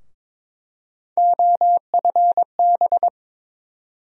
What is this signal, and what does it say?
Morse code, heard as a keyed tone, about 22 wpm, 700 Hz. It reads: OFB
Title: USA Call Sign Postfixes - 3 Letters 22wpm